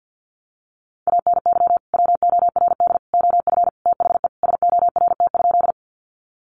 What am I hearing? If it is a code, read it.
Morse code, heard as a keyed tone, about 50 wpm, 700 Hz. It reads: ANY WORD OF THE SORT?
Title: Sentences from Top 1300 Words 50wpm